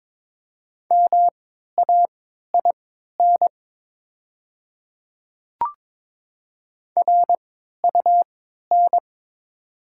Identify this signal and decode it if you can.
Morse code, heard as a keyed tone, about 22 wpm, spaced as for 13 wpm, 700 Hz. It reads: MAIN RUN
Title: 3x Character Spacing - Top 300 Words - 22wpm